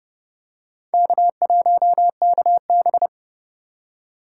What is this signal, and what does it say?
Morse code, heard as a keyed tone, about 30 wpm, 700 Hz. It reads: K1KB
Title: USA Call Signs - 4 Characters - 4x Character Spacing - 30wpm